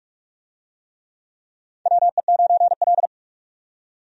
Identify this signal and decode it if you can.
Morse code, heard as a keyed tone, about 46 wpm, 700 Hz. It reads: WE9L
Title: USA Call Signs - 4 Characters - 45wpm